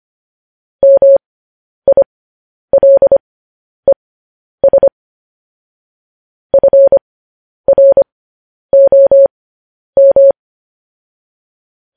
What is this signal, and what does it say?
Morse code, heard as a keyed tone, about 25 wpm, spaced as for 10 wpm, 550 Hz. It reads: MILES FROM
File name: podcast / News headlines-1x-25x10